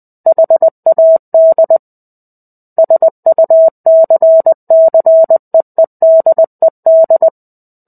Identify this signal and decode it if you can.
Morse code, heard as a keyed tone, about 20 wpm, 650 Hz. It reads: HAD SUCCEEDED
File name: 05-Chapter-05 20 WPM